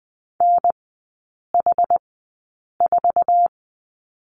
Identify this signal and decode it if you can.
Morse code, heard as a keyed tone, about 20 wpm, spaced as for 8 wpm, 700 Hz. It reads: NH4
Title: 145 Lesson 24e - Letter y - Three Characters 20wpm